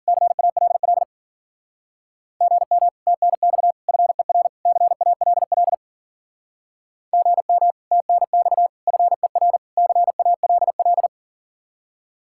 Morse code, heard as a keyed tone, about 49 wpm, 700 Hz. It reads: CALL GM TNX FER CALL GM TNX FER CALL
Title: Speed-Racing Common QSO Phrases - 40wpm